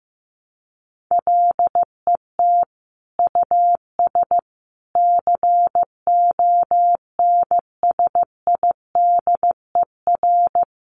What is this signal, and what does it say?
Morse code, heard as a keyed tone, about 15 wpm, 700 Hz. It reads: LET US CONSIDER
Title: Sentences from Top 600 Words 15wpm